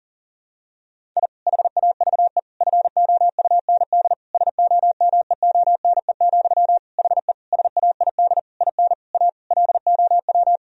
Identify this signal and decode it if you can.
Morse code, heard as a keyed tone, about 40 wpm, 700 Hz. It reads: I HAVE FOUND SOMEONE, HE SAID IN A LOW
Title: Sentences from Top 1000 Words - Review Part 1 - 40wpm